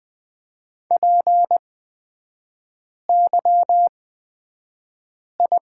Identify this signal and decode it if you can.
Morse code, heard as a keyed tone, about 20 wpm, spaced as for 5 wpm, 700 Hz. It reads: PYI
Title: 243 Lesson 39e - Letter z - Three Characters 20wpm